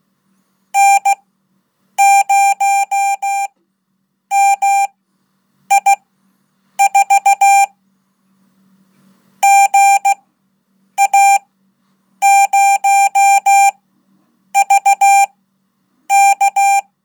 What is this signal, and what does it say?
Morse code, heard as a keyed tone, about 16 wpm, spaced as for 8 wpm, 790 Hz. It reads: N0MI4 GA0VK
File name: Code / 5min-1